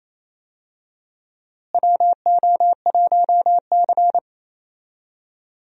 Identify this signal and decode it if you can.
Morse code, heard as a keyed tone, about 28 wpm, 700 Hz. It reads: WO1C
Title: USA Call Signs - 4 Characters - Repeated 3x - 28wpm